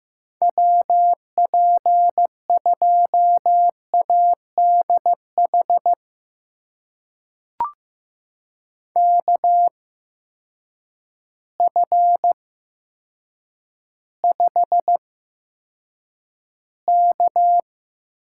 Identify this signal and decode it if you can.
Morse code, heard as a keyed tone, about 15 wpm, spaced as for 7 wpm, 700 Hz. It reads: WP2ADH KF5K
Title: USA Call Signs - 6 Characters - 8x Character Spacing - 15wpm